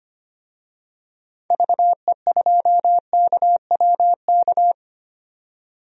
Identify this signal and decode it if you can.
Morse code, heard as a keyed tone, about 25 wpm, 700 Hz. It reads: VE2KWK
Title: DXCC-100 Call Signs - 6 Characters - 25wpm